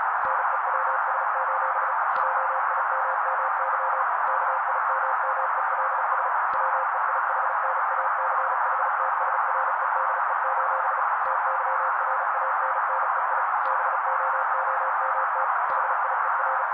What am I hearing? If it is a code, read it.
Morse code, heard as a keyed tone, about 35 wpm, 530 Hz. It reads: NEWS: JIMMY KIMMELS WEIHNACHTSREDE: TYRANNEI BOOMT HIER
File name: daily / 35wpm noise